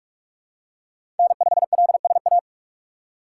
Morse code, heard as a keyed tone, about 46 wpm, 690 Hz. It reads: N5LSA
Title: USA Call Signs - 5 Characters - 4x Character Spacing - 45wpm